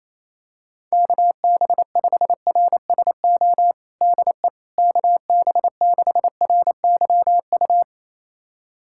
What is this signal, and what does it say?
Morse code, heard as a keyed tone, about 27 wpm, 700 Hz. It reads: KB5RSO DE KB6RYU
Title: USA Callsign DE Callsign - 6 Characters - 28wpm